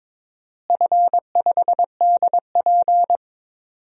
Morse code, heard as a keyed tone, about 22 wpm, 700 Hz. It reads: F5DP
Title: DXCC-100 Call Signs - 4 Characters - Repeated 3x - 22wpm